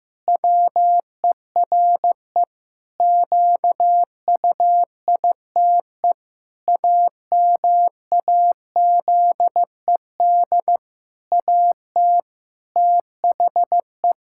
Morse code, heard as a keyed tone, about 15 wpm, 700 Hz. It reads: WERE QUITE AMAZED AT THE
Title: Pangram Challenge v2 - 15wpm